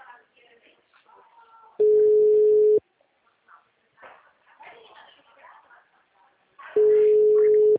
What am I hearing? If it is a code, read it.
Ringback tone (Europe, 425 Hz), with speech.